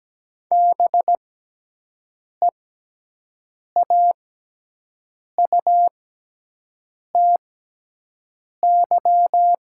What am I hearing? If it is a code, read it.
Morse code, heard as a keyed tone, about 17 wpm, spaced as for 6 wpm, 700 Hz. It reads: BEAUTY
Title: 6x Character Spacing - Top 500 Words - 17wpm